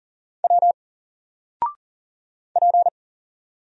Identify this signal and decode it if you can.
Morse code, heard as a keyed tone, about 39 wpm, 700 Hz. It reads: W P
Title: Single Character (Letter:Number:Punctuation:Prosign) 40wpm